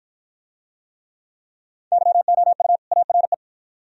Morse code, heard as a keyed tone, about 53 wpm, 700 Hz. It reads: YOU ARE